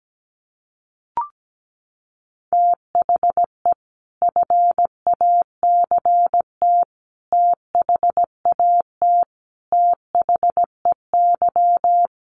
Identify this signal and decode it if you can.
Morse code, heard as a keyed tone, about 17 wpm, 700 Hz. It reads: THE FACT THAT THEY
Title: Sets of 5 Words Encore2 17wpm